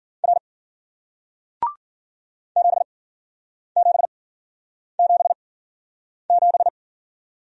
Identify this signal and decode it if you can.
Morse code, heard as a keyed tone, about 56 wpm, spaced as for 9 wpm, 700 Hz. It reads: R 7777